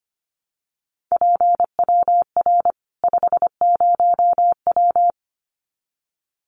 Morse code, heard as a keyed tone, about 25 wpm, 700 Hz. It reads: PWR 50W
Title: Common QSO Phrases 25wpm